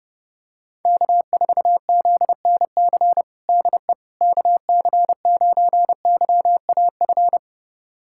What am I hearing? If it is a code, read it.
Morse code, heard as a keyed tone, about 30 wpm, 700 Hz. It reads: K4ZNC DE KC9YAF
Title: USA Callsign DE Callsign 30wpm